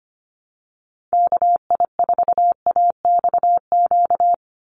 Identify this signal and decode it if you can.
Morse code, heard as a keyed tone, about 25 wpm, 690 Hz. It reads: KI4AXQ
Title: USA Call Signs 25wpm